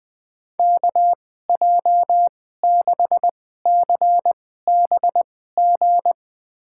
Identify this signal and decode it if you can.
Morse code, heard as a keyed tone, about 20 wpm, spaced as for 14 wpm, 700 Hz. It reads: KJ6CBG
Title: USA Call Signs - 6 Characters - 2x Character Spacing - 20wpm